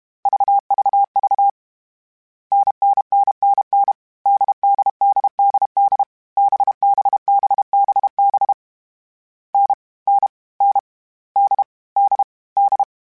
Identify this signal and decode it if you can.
Morse code, heard as a keyed tone, about 32 wpm, 800 Hz. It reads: VVV NNNNN DDDDD BBBBB N N N D D D